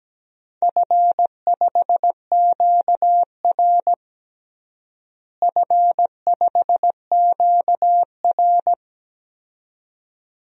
Morse code, heard as a keyed tone, about 17 wpm, 700 Hz. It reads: F5QR F5QR
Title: DXCC-100 Call Signs - 4 Characters - Repeated 2x - 17wpm